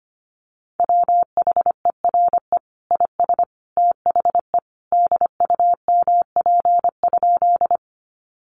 Morse code, heard as a keyed tone, about 25 wpm, 700 Hz. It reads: WHERE IS THE DUMP?
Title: Sentences from Top 1300 Words 25wpm